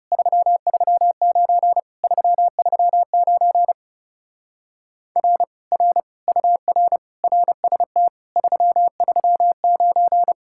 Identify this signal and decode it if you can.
Morse code, heard as a keyed tone, about 38 wpm, 700 Hz. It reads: 339 339 R R UR RST 339